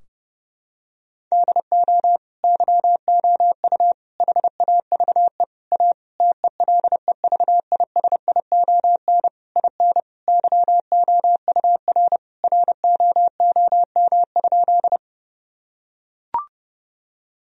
Morse code, heard as a keyed tone, about 30 wpm, 700 Hz. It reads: DO YOU HAVE A TELEVISION IN YOUR ROOM?
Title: Sentences from Top 1000 Words - Review Part 4 - 30wpm